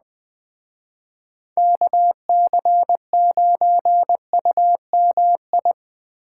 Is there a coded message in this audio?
Yes: Morse code from 1.6 to 5.7 s, heard as a keyed tone, about 20 wpm, 700 Hz. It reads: KC9UMI